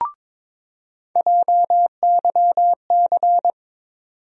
Morse code, heard as a keyed tone, about 22 wpm, 700 Hz. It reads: JYC
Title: USA Call Sign Postfixes - 3 Letters 22wpm